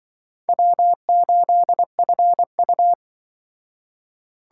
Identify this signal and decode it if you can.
Morse code, heard as a keyed tone, about 23 wpm, 700 Hz. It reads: W8FU
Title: Speed Racing - USA 1x2 Call Signs 20wpm